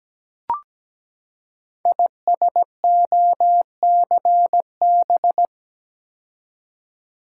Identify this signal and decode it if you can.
Morse code, heard as a keyed tone, about 17 wpm, 700 Hz. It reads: ISOCB
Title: Code Groups - 5 Letters - 17wpm